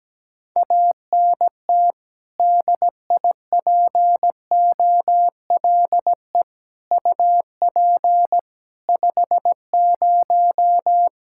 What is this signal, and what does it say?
Morse code, heard as a keyed tone, about 17 wpm, 700 Hz. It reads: ANT DIPOLE UP 50
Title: Practice QSO - Exchange 2 - Info - 17wpm